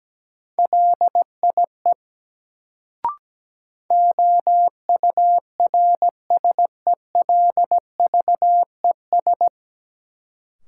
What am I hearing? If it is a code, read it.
Morse code, heard as a keyed tone, about 17 wpm, 700 Hz. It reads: LIE OURSELVES
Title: Sentences from Top 1600 Words - New Word Drill - 17wpm